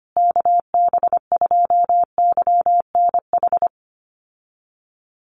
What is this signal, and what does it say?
Morse code, heard as a keyed tone, about 25 wpm, 700 Hz. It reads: KB2YNH
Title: USA Call Signs Repeated 3x 25wpm